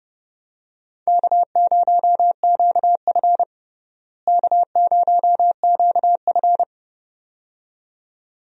Morse code, heard as a keyed tone, about 31 wpm, 700 Hz. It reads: K0QF K0QF
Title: USA Call Signs - 4 Characters - Repeated 2x - 30wpm